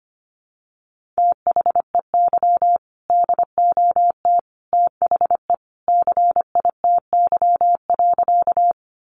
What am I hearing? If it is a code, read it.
Morse code, heard as a keyed tone, about 25 wpm, 700 Hz. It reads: THEY DOT THE CITY.
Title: Sentences from Top 1500 Words 25wpm